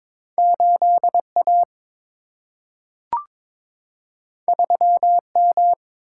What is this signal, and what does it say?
Morse code, heard as a keyed tone, about 22 wpm, 700 Hz. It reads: 8 A 3 M
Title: Single-Letter Single-Number 22wpm